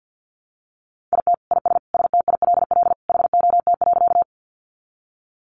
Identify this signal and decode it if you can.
Morse code, heard as a keyed tone, about 50 wpm, 700 Hz. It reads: IT IS STILL HOT.
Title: Easy Phrases 50wpm